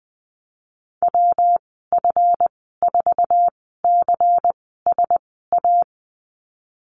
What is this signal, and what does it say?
Morse code, heard as a keyed tone, about 20 wpm, spaced as for 15 wpm, 700 Hz. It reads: WF4CSA
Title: 186 Lesson 30g - Letter b - Call Signs 20wpm